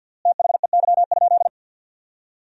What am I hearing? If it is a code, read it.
Morse code, heard as a keyed tone, about 51 wpm, 700 Hz. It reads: THEY?
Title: 3x Word-Spacing - Sentences from Top 300 Words - 50wpm